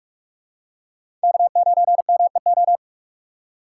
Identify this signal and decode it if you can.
Morse code, heard as a keyed tone, about 42 wpm, 700 Hz. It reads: K9MEO
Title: USA Call Signs - 5 Characters - 6x Character Spacing - 45wpm